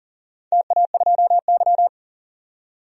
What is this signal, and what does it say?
Morse code, heard as a keyed tone, about 40 wpm, 700 Hz. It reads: TA2Y
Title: DXCC-100 Call Signs - 4 Characters - 40wpm